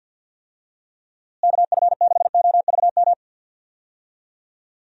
Morse code, heard as a keyed tone, about 53 wpm, 700 Hz. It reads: KF6OVM
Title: USA Call Signs - 6 Characters - Repeated 2x - 50wpm